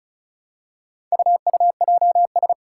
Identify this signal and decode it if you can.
Morse code, heard as a keyed tone, about 36 wpm, 700 Hz. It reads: UUJS
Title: Four Challenging Letter-Number Combinations 35wpm